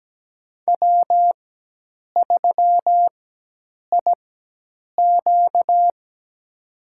Morse code, heard as a keyed tone, about 17 wpm, spaced as for 8 wpm, 700 Hz. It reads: W3IQ
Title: USA Call Signs - 4x Character Spacing - 17wpm